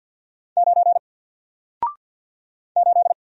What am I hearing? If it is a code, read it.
Morse code, heard as a keyed tone, about 48 wpm, 700 Hz. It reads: 9 8
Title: Single Letter-Number Alt 50wpm